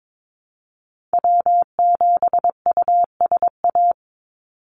Morse code, heard as a keyed tone, about 22 wpm, 700 Hz. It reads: W7USA
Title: NA SOTA Activator Callsigns 22wpm